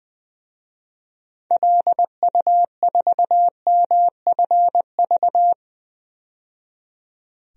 Morse code, heard as a keyed tone, about 20 wpm, 700 Hz. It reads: LU4MFV
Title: DXCC-100 Call Signs - 6 Characters - 20wpm